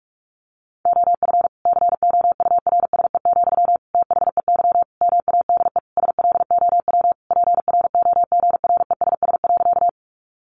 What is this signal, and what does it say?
Morse code, heard as a keyed tone, about 44 wpm, 700 Hz. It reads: OF COURSE, THEY MADE SLOW PROGRESS.
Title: Sentences from Top 1400 Words 45wpm